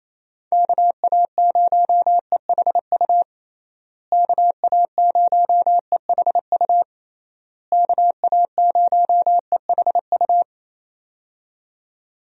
Morse code, heard as a keyed tone, about 27 wpm, 700 Hz. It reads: KA0EHU KA0EHU KA0EHU
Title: USA Call Signs - 6 Characters - Repeated 3x - 28wpm